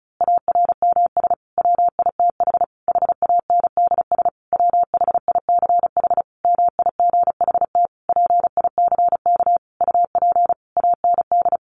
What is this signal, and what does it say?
Morse code, heard as a keyed tone, about 34 wpm, 700 Hz. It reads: ARMS WITH HANDS WHICH MIGHT PICK UP AND
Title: A Sound Of Thunder 35wpm